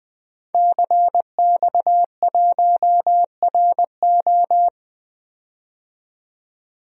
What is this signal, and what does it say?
Morse code, heard as a keyed tone, about 20 wpm, 700 Hz. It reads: CX1RO
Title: DXCC-100 Call Signs - 5 Characters - 20wpm